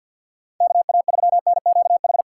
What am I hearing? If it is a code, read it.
Morse code, heard as a keyed tone, about 47 wpm, 700 Hz. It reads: KA3NQH
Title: USA Call Signs - 6 Characters - 6x Character Spacing - 50wpm